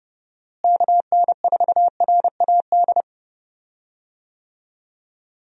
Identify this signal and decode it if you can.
Morse code, heard as a keyed tone, about 30 wpm, 700 Hz. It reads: KN4RAD
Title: USA Call Signs - 6 Characters - Repeated 2x - 30wpm